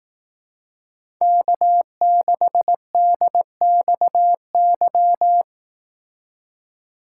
Morse code, heard as a keyed tone, about 18 wpm, 700 Hz. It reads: K6DXY
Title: Speed Racing - USA 1x3 Call Signs 15wpm